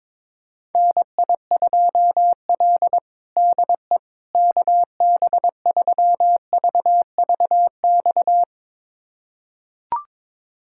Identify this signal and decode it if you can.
Morse code, heard as a keyed tone, about 22 wpm, 700 Hz. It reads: NI2L DE KB3VVX
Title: USA Callsign DE Callsign 22wpm